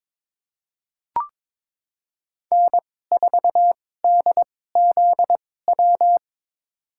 Morse code, heard as a keyed tone, about 22 wpm, spaced as for 16 wpm, 700 Hz. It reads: N4DZW
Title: USA Call Signs - 2x Character Spacing - 22wpm